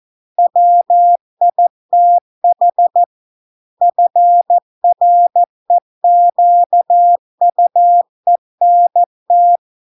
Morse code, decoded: WITH FREQUENT

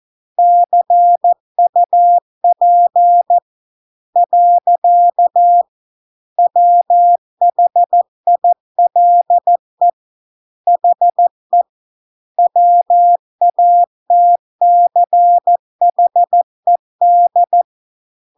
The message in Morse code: CUP . WHILE HE WATCHED

